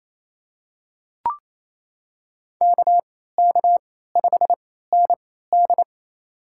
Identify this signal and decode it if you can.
Morse code, heard as a keyed tone, about 28 wpm, spaced as for 16 wpm, 700 Hz. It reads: KK5ND
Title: USA Call Signs - 3x Character Spacing - 28wpm